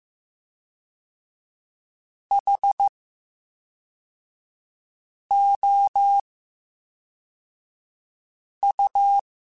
Morse code, heard as a keyed tone, about 15 wpm, 800 Hz. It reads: H O U
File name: Special INGs / Special INGs-mp3s / VerySlowToFaster Ing @10